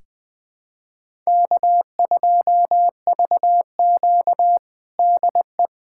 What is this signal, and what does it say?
Morse code, heard as a keyed tone, about 20 wpm, 700 Hz. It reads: K2VQ DE
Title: USA Callsign DE Callsign - 4 Characters - 20wpm